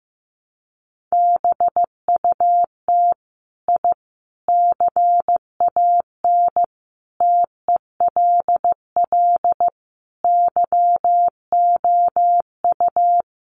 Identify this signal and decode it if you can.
Morse code, heard as a keyed tone, about 15 wpm, 700 Hz. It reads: BUT I CAN TELL YOU
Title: Sets of 5 Words Encore 15wpm